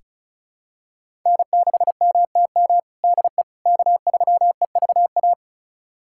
Morse code, heard as a keyed tone, about 34 wpm, 700 Hz. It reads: N6MTM DE K3EVA
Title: USA Callsign DE Callsign - 5 Characters - 35wpm